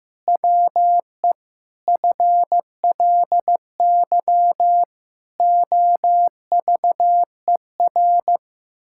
Morse code, heard as a keyed tone, about 15 wpm, 700 Hz. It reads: WE FLY OVER
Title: Pangram Challenge v2 - 15wpm